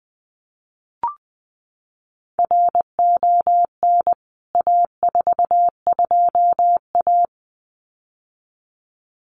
Morse code, heard as a keyed tone, about 20 wpm, 700 Hz. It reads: RON A42A
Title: CWT Exchange 20wpm